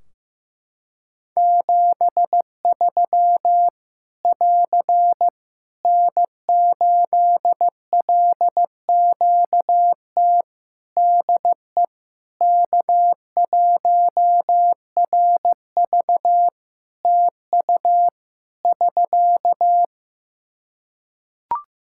Morse code, heard as a keyed tone, about 15 wpm, 700 Hz. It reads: 73 <AR> N8LQT DE K1RV TU <SK>
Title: K1USN SST QSO Practice 15wpm